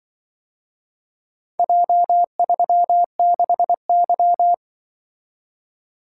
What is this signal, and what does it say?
Morse code, heard as a keyed tone, about 24 wpm, 700 Hz. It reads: J36Y
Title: Speed-Racing - Four Letter-Number Combinations - 20wpm